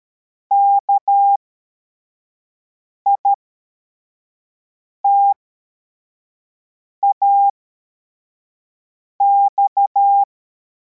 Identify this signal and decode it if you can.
Morse code, heard as a keyed tone, about 13 wpm, spaced as for 4 wpm, 800 Hz. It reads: KITAX